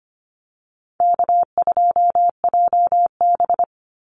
Morse code, heard as a keyed tone, about 25 wpm, 700 Hz. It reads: K2JB
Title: NA SOTA Chaser Callsigns 25wpm